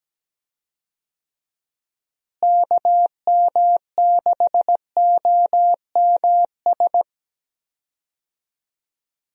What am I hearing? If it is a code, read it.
Morse code, heard as a keyed tone, about 17 wpm, 700 Hz. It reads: KM6OMS
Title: USA Call Signs - 6 Characters - Repeated 2x - 17wpm